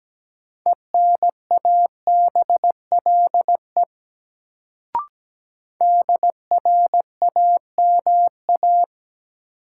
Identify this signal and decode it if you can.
Morse code, heard as a keyed tone, about 17 wpm, 700 Hz. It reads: ENABLE DRAMA
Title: Sentences from Top 1500 Words - New Word Drill - 17wpm